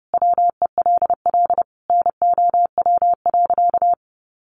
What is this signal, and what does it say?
Morse code, heard as a keyed tone, about 30 wpm, 700 Hz. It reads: WELL NOW.
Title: Sentences from Top 1000 Words 30wpm